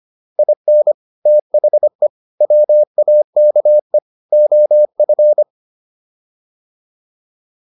Morse code, heard as a keyed tone, about 25 wpm, 600 Hz. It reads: IN THE WAKE OF